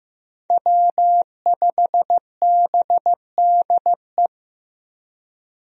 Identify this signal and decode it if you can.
Morse code, heard as a keyed tone, about 15 wpm, 700 Hz. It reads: W5BDE
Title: USA Call Signs - 5 Characters - Repeated 2x - 15wpm